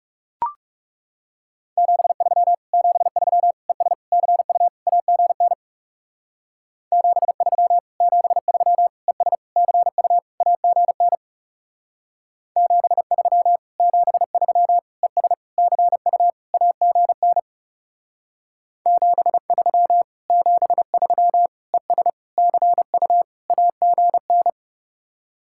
Morse code, heard as a keyed tone, about 39 wpm, 700 Hz. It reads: 73 73 ES CU AGN 73 73 ES CU AGN 73 73 ES CU AGN 73 73 ES CU AGN